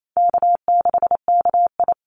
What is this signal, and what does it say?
Morse code, heard as a keyed tone, about 28 wpm, 700 Hz. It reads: K6KI
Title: Easy Call Signs - 28wpm